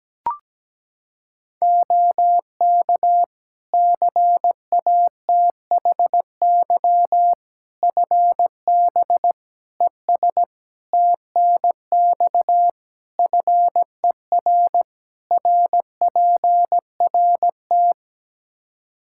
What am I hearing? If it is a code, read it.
Morse code, heard as a keyed tone, about 17 wpm, 700 Hz. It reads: OK CATHY FB ES TNX FER RPRT
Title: Practice QSO - Exchange 2 - Info - 17wpm